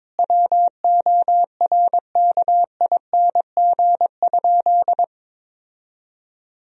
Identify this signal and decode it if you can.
Morse code, heard as a keyed tone, about 22 wpm, 700 Hz. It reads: WORKING?
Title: Sentences from Top 1000 Words - Review Part 4 - 22wpm